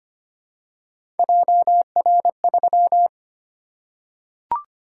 Morse code, heard as a keyed tone, about 25 wpm, 700 Hz. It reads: JR3
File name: Three Letter-Number Combinations - 25wpm